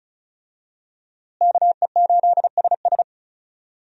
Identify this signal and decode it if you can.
Morse code, heard as a keyed tone, about 36 wpm, 700 Hz. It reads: KE8SS
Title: USA Call Signs - 5 Characters - Repeated 2x - 35wpm